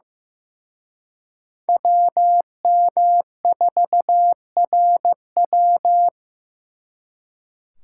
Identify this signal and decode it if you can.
Morse code, heard as a keyed tone, about 18 wpm, 700 Hz. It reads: WM4RW
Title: Speed Racing - USA 2x2 Call Signs 15wpm